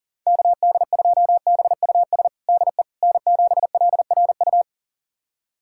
Morse code, heard as a keyed tone, about 40 wpm, 700 Hz. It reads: KD2BUS DE N7LRU
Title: USA Callsign DE Callsign 40wpm